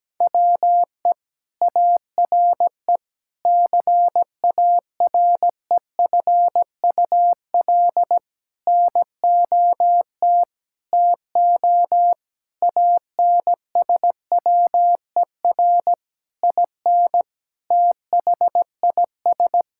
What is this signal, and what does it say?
Morse code, heard as a keyed tone, about 17 wpm, 700 Hz. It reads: WE ARE CAREFUL NOT TO ANSWER IN THIS
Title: Sentences from Top 1600 Words 17wpm